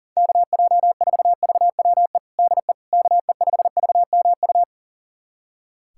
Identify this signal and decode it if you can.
Morse code, heard as a keyed tone, about 40 wpm, 700 Hz. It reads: KJ4VWE DE KE5VMU
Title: USA Callsign DE Callsign - 6 Characters - 40wpm